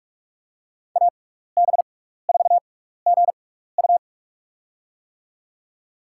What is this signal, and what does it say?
Morse code, heard as a keyed tone, about 44 wpm, spaced as for 15 wpm, 700 Hz. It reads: AB4GU
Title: USA Call Signs - 5 Characters - 6x Character Spacing - 45wpm